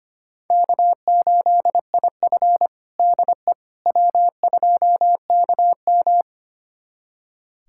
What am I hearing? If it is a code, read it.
Morse code, heard as a keyed tone, about 25 wpm, 700 Hz. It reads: K8IF DE W2KM
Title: USA Callsign DE Callsign - 4 Characters - 25wpm